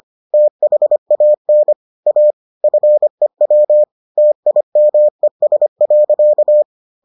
Morse code, heard as a keyed tone, about 25 wpm, 600 Hz. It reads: THAN A FEW TIMES.